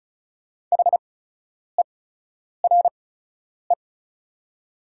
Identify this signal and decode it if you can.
Morse code, heard as a keyed tone, about 36 wpm, spaced as for 10 wpm, 700 Hz. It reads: HERE
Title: 8x Character Spacing - Top 200 Words - 35wpm